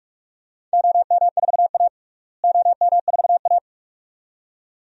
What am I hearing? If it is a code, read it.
Morse code, heard as a keyed tone, about 44 wpm, 700 Hz. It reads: OM4A OM4A
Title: DXCC-100 Call Signs - 4 Characters - Repeated 2x - 45wpm